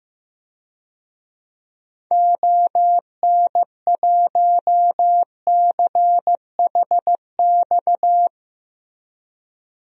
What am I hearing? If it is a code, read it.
Morse code, heard as a keyed tone, about 16 wpm, 700 Hz. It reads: ON1CHX
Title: DXCC-100 Call Signs - 6 Characters - Repeated 2x - 15wpm